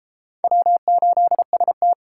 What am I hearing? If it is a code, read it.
Morse code, heard as a keyed tone, about 33 wpm, 700 Hz. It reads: W8ST